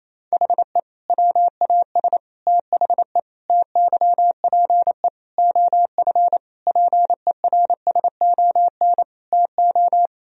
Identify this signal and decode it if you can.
Morse code, heard as a keyed tone, about 28 wpm, 700 Hz. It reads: HE WAS THE TYPE OF PERSON TO
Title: Sentences from Top 1000 Words - Review Part 3 - 28wpm